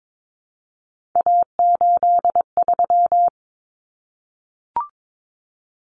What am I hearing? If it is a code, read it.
Morse code, heard as a keyed tone, about 22 wpm, 700 Hz. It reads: A83
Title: Common Cut Numbers - 3-Digit 22wpm